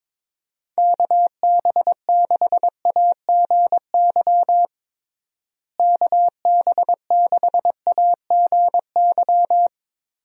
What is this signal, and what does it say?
Morse code, heard as a keyed tone, about 21 wpm, 700 Hz. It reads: KB6AGY KB6AGY
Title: USA Call Signs - 6 Characters - Repeated 3x - 22wpm